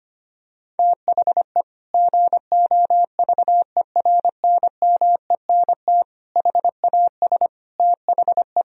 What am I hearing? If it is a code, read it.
Morse code, heard as a keyed tone, about 25 wpm, 700 Hz. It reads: THE GOVERNMENT HAS THE
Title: Sentences from Top 1000 Words - Review Part 3 - 25wpm